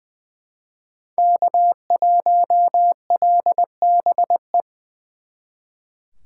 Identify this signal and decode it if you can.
Morse code, heard as a keyed tone, about 24 wpm, 700 Hz. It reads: K1LBE
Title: Speed Racing - USA Call Signs 20wpm